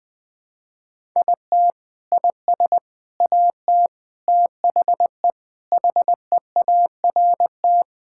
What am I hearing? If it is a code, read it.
Morse code, heard as a keyed tone, about 21 wpm, 700 Hz. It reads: IT IS AT THE HEART
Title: Sentences with 4-Word Phrases - 20wpm